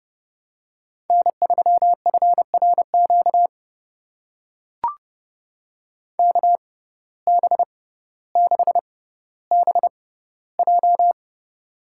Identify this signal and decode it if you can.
Morse code, heard as a keyed tone, about 30 wpm, spaced as for 15 wpm, 700 Hz. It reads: N3FRQ KB6BJ
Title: USA Call Signs - 6x Character Spacing - 30wpm